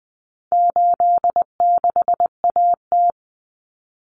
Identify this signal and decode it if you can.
Morse code, heard as a keyed tone, about 20 wpm, 700 Hz. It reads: 86AT
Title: Common Cut Numbers - 4-Digit 20wpm